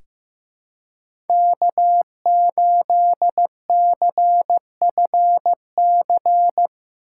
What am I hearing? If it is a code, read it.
Morse code, heard as a keyed tone, about 15 wpm, 700 Hz. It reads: K8CFC
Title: USA Call Signs - 5 Characters - 3x Character Spacing - 15wpm